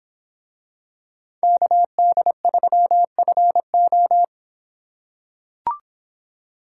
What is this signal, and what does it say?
Morse code, heard as a keyed tone, about 26 wpm, 700 Hz. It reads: KD3FO